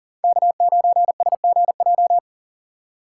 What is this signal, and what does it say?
Morse code, heard as a keyed tone, about 52 wpm, 700 Hz. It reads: K9SGJ